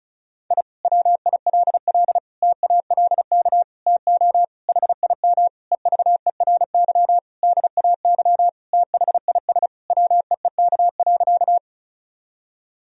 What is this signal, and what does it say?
Morse code, heard as a keyed tone, about 36 wpm, 700 Hz. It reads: I WILL TALK TO HIM EVERY DAY THIS WEEK.
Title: Sentences from Top 1000 Words - Review Part 2 - 35wpm